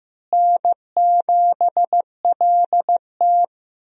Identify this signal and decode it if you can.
Morse code, heard as a keyed tone, about 15 wpm, 700 Hz. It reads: N7LT
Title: USA Call Signs - 4 Characters - Repeated 2x - 15wpm